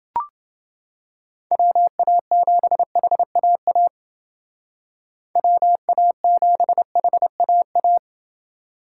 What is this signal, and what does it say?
Morse code, heard as a keyed tone, about 29 wpm, 700 Hz. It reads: WA7HAA WA7HAA